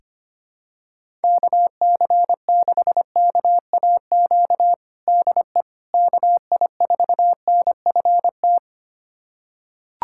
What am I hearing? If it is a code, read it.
Morse code, heard as a keyed tone, about 25 wpm, 700 Hz. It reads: KC6KAQ DE KI4NFT
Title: USA Callsign DE Callsign - 6 Characters - 25wpm